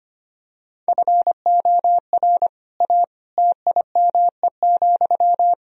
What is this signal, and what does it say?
Morse code, heard as a keyed tone, about 25 wpm, 700 Hz. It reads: FOR A TIME,
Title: Sentences from Top 1000 Words - Review Part 4 - 25wpm